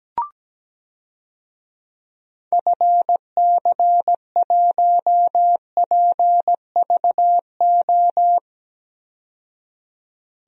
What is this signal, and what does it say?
Morse code, heard as a keyed tone, about 17 wpm, 700 Hz. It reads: FC1PVO
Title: DXCC-100 Call Signs - 6 Characters - 17wpm